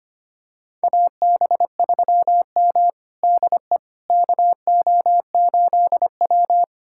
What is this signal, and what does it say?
Morse code, heard as a keyed tone, about 25 wpm, 700 Hz. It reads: AB3M DE KO8W
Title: USA Callsign DE Callsign - 4 Characters - 25wpm